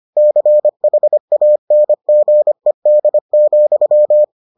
Morse code, heard as a keyed tone, about 25 wpm, 600 Hz. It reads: CHANGED,